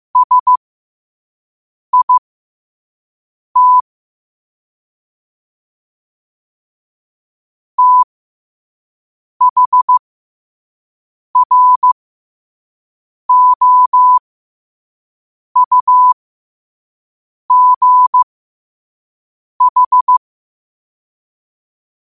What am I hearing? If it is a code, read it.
Morse code, heard as a keyed tone, about 15 wpm, spaced as for 5 wpm, 1000 Hz. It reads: SIT THROUGH
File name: 5wpm